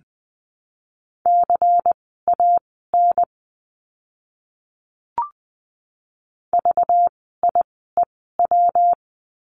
Morse code, heard as a keyed tone, about 20 wpm, spaced as for 15 wpm, 700 Hz. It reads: CAN VIEW
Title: 192 Lesson 31f - Letter v - Words 20wpm